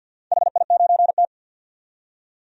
Morse code, heard as a keyed tone, about 51 wpm, 700 Hz. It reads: HI9T
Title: DXCC-100 Call Signs - 4 Characters - 50wpm